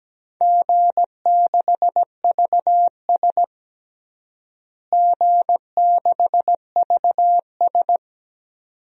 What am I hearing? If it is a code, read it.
Morse code, heard as a keyed tone, about 17 wpm, 700 Hz. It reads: G6VS G6VS
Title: DXCC-100 Call Signs - 4 Characters - Repeated 2x - 17wpm